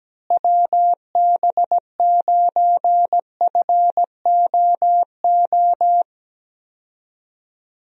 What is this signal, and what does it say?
Morse code, heard as a keyed tone, about 17 wpm, 700 Hz. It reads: WB9FOO
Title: USA Call Signs - 6 Characters - Repeated 3x - 17wpm